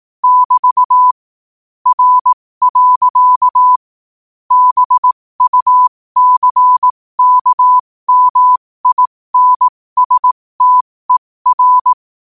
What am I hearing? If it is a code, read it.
Morse code, heard as a keyed tone, about 18 wpm, spaced as for 15 wpm, 1000 Hz. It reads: <BT> R. BUCKMINSTER
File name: qotdfiles / 15wpm